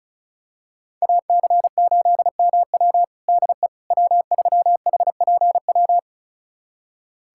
Morse code, heard as a keyed tone, about 34 wpm, 700 Hz. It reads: AC8MW DE W3HPW
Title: USA Callsign DE Callsign - 5 Characters - 35wpm